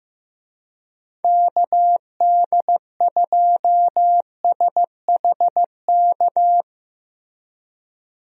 Morse code, heard as a keyed tone, about 15 wpm, 700 Hz. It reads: KD2SHK
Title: USA Call Signs - 6 Characters - Repeated 3x - 15wpm